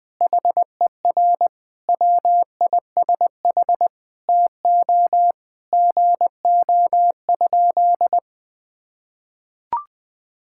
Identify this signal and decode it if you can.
Morse code, heard as a keyed tone, about 20 wpm, 700 Hz. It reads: HER WISH TO GO?
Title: Sentences from Top 1600 Words 20wpm